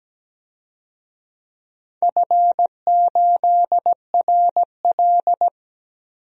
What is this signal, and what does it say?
Morse code, heard as a keyed tone, about 17 wpm, 700 Hz. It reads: F8RL
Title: DXCC-100 Call Signs - 4 Characters - 17wpm